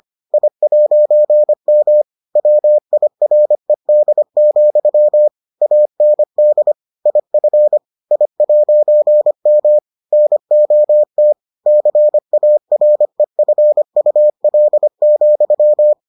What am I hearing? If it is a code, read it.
Morse code, heard as a keyed tone, about 25 wpm, 600 Hz. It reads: I'M WIRED, AND IF I'M NOT CAREFUL,